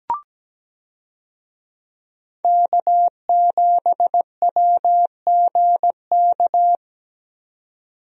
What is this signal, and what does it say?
Morse code, heard as a keyed tone, about 17 wpm, 700 Hz. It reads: K7WGK
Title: USA Call Signs - 5 Characters - 17wpm